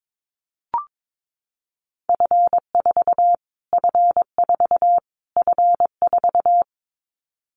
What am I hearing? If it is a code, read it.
Morse code, heard as a keyed tone, about 22 wpm, 700 Hz. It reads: F4 F4 F4
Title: Callsign Prefixes for Top 25 DXCC - Repeated 3x - 22wpm